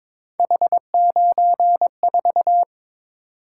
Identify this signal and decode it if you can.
Morse code, heard as a keyed tone, about 26 wpm, 700 Hz. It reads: H94
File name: Speed-Racing - Three Letter-Number Combinations - 22wpm